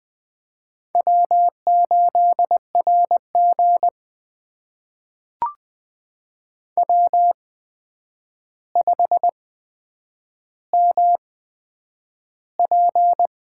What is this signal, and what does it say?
Morse code, heard as a keyed tone, about 20 wpm, spaced as for 9 wpm, 700 Hz. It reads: W8RG W5MP
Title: USA Call Signs - 4 Characters - 8x Character Spacing - 20wpm